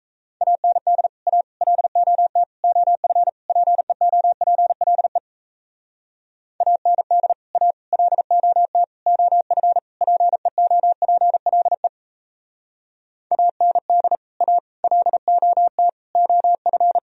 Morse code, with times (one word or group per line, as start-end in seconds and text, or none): 0.4-1.1 AND
1.3-1.4 A
1.6-2.4 LOT
2.6-3.3 OF
3.5-5.2 PEOPLE
6.6-7.3 AND
7.5-7.7 A
7.9-8.8 LOT
9.1-9.8 OF
10.0-11.9 PEOPLE
13.3-14.2 AND
14.4-14.6 A
14.8-15.9 LOT
16.2-17.0 OF